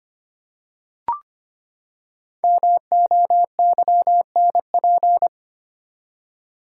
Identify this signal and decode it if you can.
Morse code, heard as a keyed tone, about 25 wpm, 700 Hz. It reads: MOYNP
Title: Code Groups - 5 Letter:Numbers - 25wpm